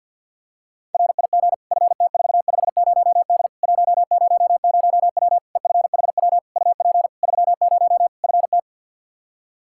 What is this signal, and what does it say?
Morse code, heard as a keyed tone, about 52 wpm, 700 Hz. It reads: RIG FT450D 100W EFHW UP 30 FT
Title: K1USN SST QSO Practice 50wpm